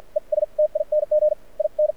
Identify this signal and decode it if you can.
Morse code, heard as a keyed tone, about 52 wpm, spaced as for 41 wpm, 590 Hz, with radio noise. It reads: ESTING IN
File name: archived / CRSnet